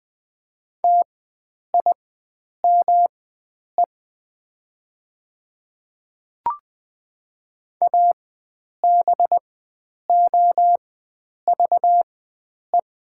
Morse code, heard as a keyed tone, about 20 wpm, spaced as for 9 wpm, 700 Hz. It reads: TIME ABOVE
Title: 4x Character Spacing - Top 500 Words - 20wpm